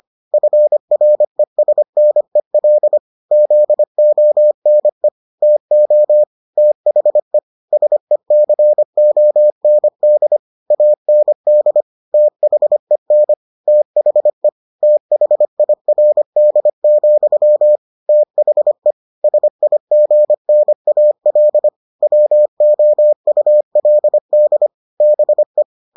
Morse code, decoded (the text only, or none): FRESNEL ZONE TO THE SECOND AND THEN THE THIRD, THE SIGNAL WOULD BE